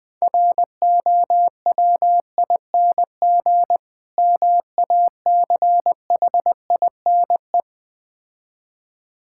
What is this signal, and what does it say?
Morse code, heard as a keyed tone, about 20 wpm, 700 Hz. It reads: ROWING MACHINE
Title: Difficult Words - Speed-Racing with Hints - 20wpm